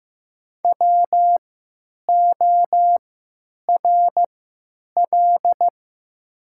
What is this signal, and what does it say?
Morse code, heard as a keyed tone, about 15 wpm, spaced as for 9 wpm, 700 Hz. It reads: WORL
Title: 3x Character Spacing - Top 200 Words - 15wpm